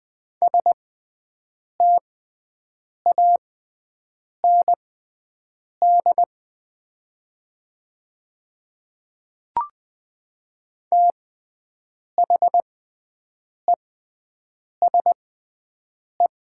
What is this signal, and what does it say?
Morse code, heard as a keyed tone, about 20 wpm, spaced as for 7 wpm, 700 Hz. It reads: STAND THESE
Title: 6x Character Spacing - Top 300 Words - 20wpm